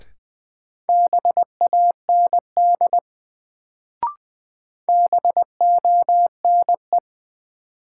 Morse code, heard as a keyed tone, about 20 wpm, 700 Hz. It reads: BAND BONE
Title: Sentences from Top 1600 Words - New Word Drill Alt - 20wpm